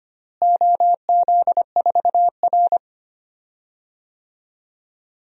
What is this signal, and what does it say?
Morse code, heard as a keyed tone, about 25 wpm, 700 Hz. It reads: OZ4R
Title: DXCC-100 Call Signs - 4 Characters - Repeated 2x - 25wpm